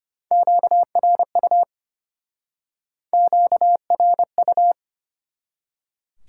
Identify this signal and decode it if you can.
Morse code, heard as a keyed tone, about 31 wpm, 700 Hz. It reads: QRU QRU
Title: Speed-Racing Top 90 QSO Elements 25wpm